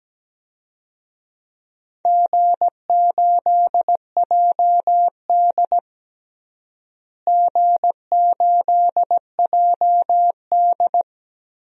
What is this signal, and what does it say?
Morse code, heard as a keyed tone, about 17 wpm, 700 Hz. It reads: G8JD G8JD
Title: DXCC-100 Call Signs - 4 Characters - Repeated 2x - 17wpm